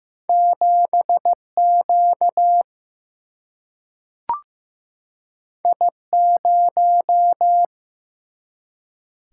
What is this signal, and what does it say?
Morse code, heard as a keyed tone, about 15 wpm, 700 Hz. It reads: 7Q I0
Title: Single-Letter Single-Number 15wpm